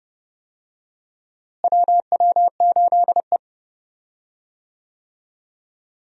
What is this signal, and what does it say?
Morse code, heard as a keyed tone, about 32 wpm, 700 Hz. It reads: WW8E
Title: USA Call Signs - 4 Characters - Repeated 3x - 30wpm